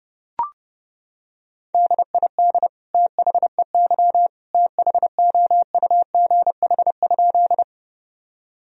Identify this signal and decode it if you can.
Morse code, heard as a keyed tone, about 32 wpm, 700 Hz. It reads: DID THEY THOUGH?
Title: Sentences from Top 1000 Words - Review Part 2 - 30wpm